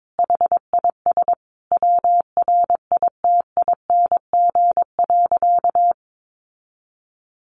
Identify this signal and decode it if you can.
Morse code, heard as a keyed tone, about 22 wpm, 700 Hz. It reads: HIS WRITING.
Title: Sentences from Top 800 Words 22wpm